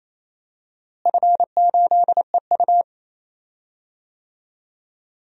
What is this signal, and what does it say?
Morse code, heard as a keyed tone, about 28 wpm, 700 Hz. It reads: F8EU
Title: DXCC-100 Call Signs - 4 Characters - Repeated 2x - 28wpm